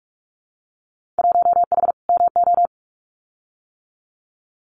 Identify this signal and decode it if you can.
Morse code, heard as a keyed tone, about 46 wpm, 700 Hz. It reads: 1H MO
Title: Winter Field Day Exchanges 45wpm